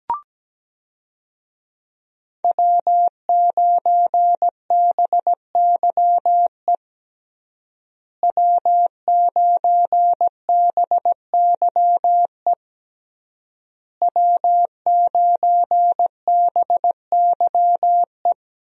Morse code, heard as a keyed tone, about 17 wpm, 700 Hz. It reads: W9BYE W9BYE W9BYE
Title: USA Call Signs - 5 Characters - Repeated 3x - 17wpm